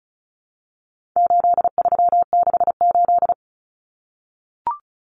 Morse code, heard as a keyed tone, about 34 wpm, 700 Hz. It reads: 8368
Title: Numbers - 4-Digit 35wpm